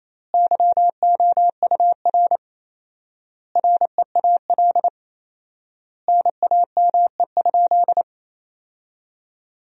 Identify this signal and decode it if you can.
Morse code, heard as a keyed tone, about 27 wpm, 700 Hz. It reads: YOUR REAL NAME?
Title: 4x Word-Spacing - Sentences from Top 300 Words - 28wpm